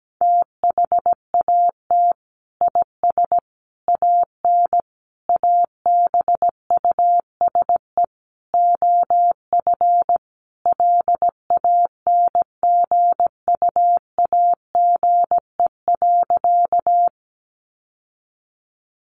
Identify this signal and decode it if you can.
Morse code, heard as a keyed tone, about 17 wpm, 700 Hz. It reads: THAT IS AN ABUSE OF LANGUAGE.
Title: Sentences from Top 1300 Words 17wpm